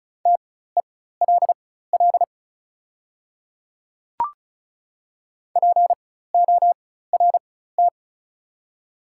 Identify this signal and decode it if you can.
Morse code, heard as a keyed tone, about 35 wpm, spaced as for 16 wpm, 700 Hz. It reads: TELL PORT
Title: 4x Character Spacing - Top 200 Words - 35wpm